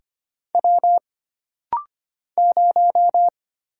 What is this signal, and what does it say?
Morse code, heard as a keyed tone, about 25 wpm, 700 Hz. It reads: W 0
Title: Single Letter-Number Alt - Rapid-Fire 25wpm